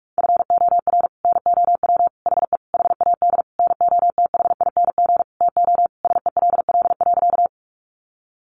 Morse code, heard as a keyed tone, about 44 wpm, 700 Hz. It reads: FOR NOW HE HAD NOTHING TO SELL.
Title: Sentences from Top 1100 Words 45wpm